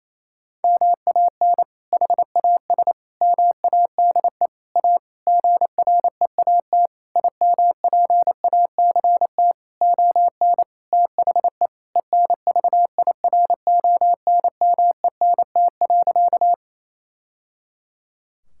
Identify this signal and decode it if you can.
Morse code, heard as a keyed tone, about 27 wpm, 700 Hz. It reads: MAN HAS MADE A GREAT IMPACT ON THE ENVIRONMENT.
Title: Sentences from Top 1000 Words - Review Part 6 - 28wpm